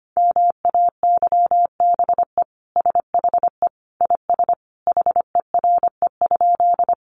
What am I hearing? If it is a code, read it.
Morse code, heard as a keyed tone, about 25 wpm, 700 Hz. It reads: MAYBE SHE IS HERE?
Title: Sentences from Top 1000 Words 25wpm